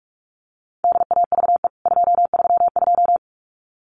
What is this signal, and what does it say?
Morse code, heard as a keyed tone, about 44 wpm, 700 Hz. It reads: DAVE 232
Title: CWT Exchange 45wpm